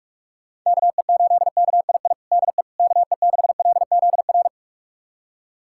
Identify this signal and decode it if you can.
Morse code, heard as a keyed tone, about 46 wpm, 700 Hz. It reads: KE8KII DE KE6LZR
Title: USA Callsign DE Callsign - 6 Characters - 45wpm